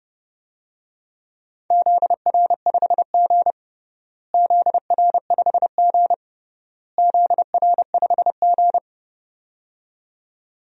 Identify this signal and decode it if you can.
Morse code, heard as a keyed tone, about 30 wpm, 700 Hz. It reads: ZR5G ZR5G ZR5G
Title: DXCC-100 Call Signs - 4 Characters - Repeated 3x - 30wpm